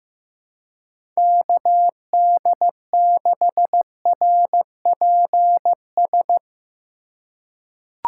Morse code, heard as a keyed tone, about 15 wpm, 700 Hz. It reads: KD6RPS
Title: USA Call Signs - 6 Characters - 3x Character Spacing - 15wpm